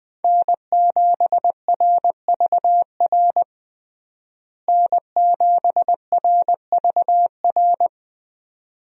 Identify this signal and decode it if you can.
Morse code, heard as a keyed tone, about 20 wpm, 700 Hz. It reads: N7RVR N7RVR
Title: USA Call Signs - 5 Characters - Repeated 2x - 20wpm